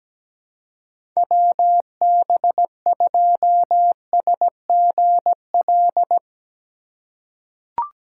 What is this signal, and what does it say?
Morse code, heard as a keyed tone, about 17 wpm, 700 Hz. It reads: WB2SGL
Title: USA Call Signs - 6 Characters - 2x Character Spacing - 17wpm